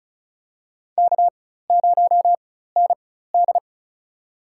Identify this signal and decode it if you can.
Morse code, heard as a keyed tone, about 36 wpm, spaced as for 16 wpm, 700 Hz. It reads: K0ND
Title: USA Call Signs - 4x Character Spacing - 35wpm